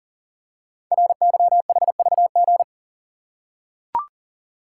Morse code, heard as a keyed tone, about 40 wpm, 700 Hz. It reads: RYHVG
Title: Code Groups - 5 Letters - 40wpm